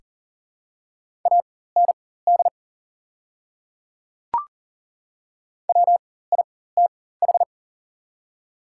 Morse code, heard as a keyed tone, about 40 wpm, spaced as for 19 wpm, 690 Hz. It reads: AND WITH
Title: 4x Character Spacing - Top 200 Words - 40wpm